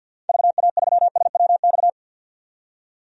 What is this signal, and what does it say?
Morse code, heard as a keyed tone, about 47 wpm, 700 Hz. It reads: VA3SWX
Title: DXCC-100 Call Signs - 6 Characters - 50wpm